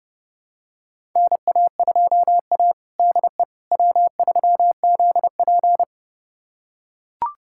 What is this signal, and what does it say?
Morse code, heard as a keyed tone, about 30 wpm, 700 Hz. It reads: NA2A DE W3ZP
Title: USA Callsign DE Callsign - 4 Characters - 30wpm